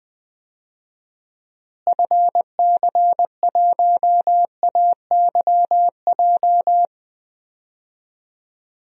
Morse code, heard as a keyed tone, about 20 wpm, 700 Hz. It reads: FC1AYJ
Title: DXCC-100 Call Signs - 6 Characters - Repeated 3x - 20wpm